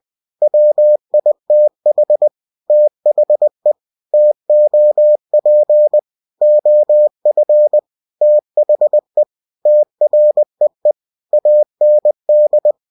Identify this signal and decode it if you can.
Morse code, heard as a keyed tone, about 20 wpm, 600 Hz. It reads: WITH THE TOP OF THE TREE AND
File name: cw-books / winnie-the-pooh / winnie-20wpm-page-0006